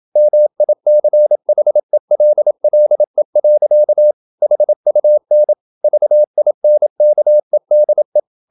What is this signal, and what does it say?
Morse code, heard as a keyed tone, about 27 wpm, 600 Hz. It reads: MICHELLE. HUN VINKEDE
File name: Ferie QRQ 27wpm